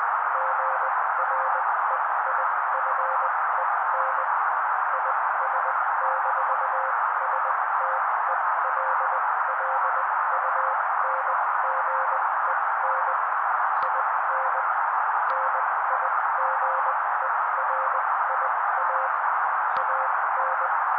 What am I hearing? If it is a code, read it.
Morse code, heard as a keyed tone, about 20 wpm, spaced as for 16 wpm, 530 Hz. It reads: GREIFEN IS-STELLUNGEN IN NIGERIA AN
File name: daily / 16wpm noise